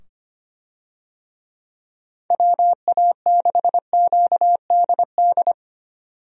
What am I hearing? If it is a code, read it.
Morse code, heard as a keyed tone, about 25 wpm, 700 Hz. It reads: WA6QDD
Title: USA Call Signs - 6 Characters - 25wpm